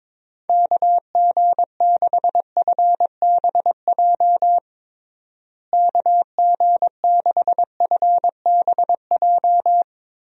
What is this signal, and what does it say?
Morse code, heard as a keyed tone, about 22 wpm, 700 Hz. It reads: KG6FBJ KG6FBJ
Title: USA Call Signs - 6 Characters - Repeated 2x - 22wpm